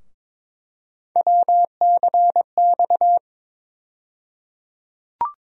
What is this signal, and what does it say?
Morse code, heard as a keyed tone, about 22 wpm, 700 Hz. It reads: WCX
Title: USA Call Sign Postfixes - 3 Letters 22wpm